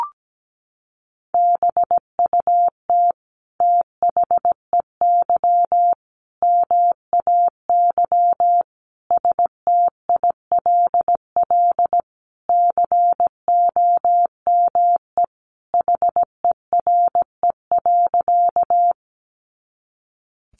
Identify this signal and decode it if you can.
Morse code, heard as a keyed tone, about 17 wpm, 700 Hz. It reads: BUT THEY MAY STILL COME HERE.
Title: Sentences from Top 300 Words 17wpm